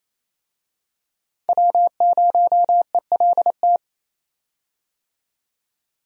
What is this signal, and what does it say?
Morse code, heard as a keyed tone, about 28 wpm, 700 Hz. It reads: W0ELT
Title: USA Call Signs - 5 Characters - 28wpm